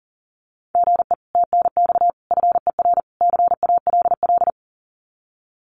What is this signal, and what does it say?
Morse code, heard as a keyed tone, about 40 wpm, 700 Hz. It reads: GE TNX FER CALL
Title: Common QSO Phrases Encore 40wpm